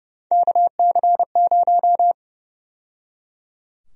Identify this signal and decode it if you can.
Morse code, heard as a keyed tone, about 30 wpm, 700 Hz. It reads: KC0
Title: USA Call Sign Prefixes - 30wpm